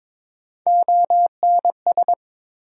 Morse code, heard as a keyed tone, about 22 wpm, 700 Hz. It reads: ONS
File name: ARRL Sections v2 - 22wpm